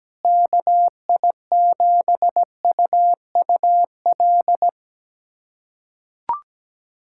Morse code, heard as a keyed tone, about 17 wpm, 700 Hz. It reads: KI7UUL
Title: USA Callsign DE Callsign 17wpm